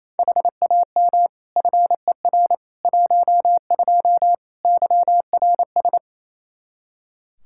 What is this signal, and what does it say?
Morse code, heard as a keyed tone, about 28 wpm, 700 Hz. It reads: HAM FER 12 YRS
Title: Practice QSO - Exchange 3 - More Info - 28wpm